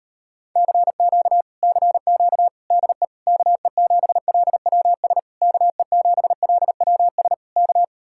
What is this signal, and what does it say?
Morse code, heard as a keyed tone, about 38 wpm, 700 Hz. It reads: CQ CQ DE KE7LWS KE7LWS K